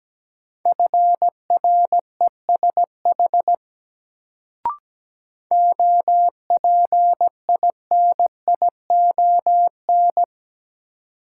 Morse code, heard as a keyed tone, about 17 wpm, 700 Hz. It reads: FRESH OPINION
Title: Sentences from Top 1200 Words - New Word Drill - 17wpm